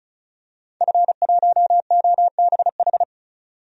Morse code, heard as a keyed tone, about 34 wpm, 700 Hz. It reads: F1OBH
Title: DXCC-100 Call Signs - 5 Characters - 35wpm